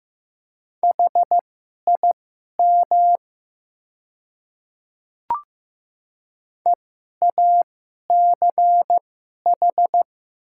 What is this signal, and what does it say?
Morse code, heard as a keyed tone, about 15 wpm, spaced as for 11 wpm, 700 Hz. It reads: HIM EACH
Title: 2x Character Spacing - Top 300 Words - 15wpm